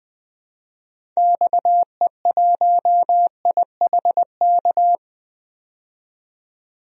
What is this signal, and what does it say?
Morse code, heard as a keyed tone, about 20 wpm, 700 Hz. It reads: XE1IHK
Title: DXCC-100 Call Signs - 6 Characters - 20wpm